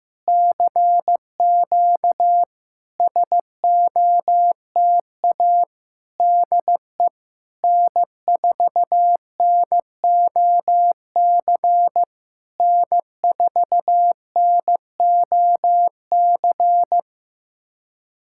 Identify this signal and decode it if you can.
Morse code, heard as a keyed tone, about 15 wpm, 700 Hz. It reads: CQ SOTA DE N4NOC N4NOC
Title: SOTA QSO Practice 15wpm